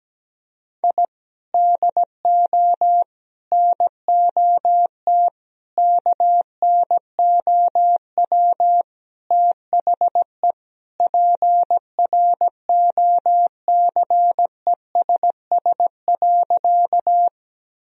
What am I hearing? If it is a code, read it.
Morse code, heard as a keyed tone, about 17 wpm, 700 Hz. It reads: I DO NOT KNOW THE PROCESS.
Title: Sentences from Top 1000 Words - Review Part 3 - 17wpm